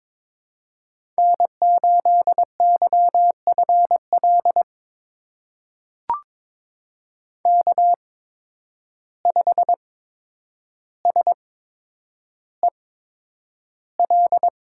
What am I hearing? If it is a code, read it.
Morse code, heard as a keyed tone, about 22 wpm, spaced as for 9 wpm, 700 Hz. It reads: N8YFL K5SEL
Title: USA Call Signs - 5 Characters - 8x Character Spacing - 22wpm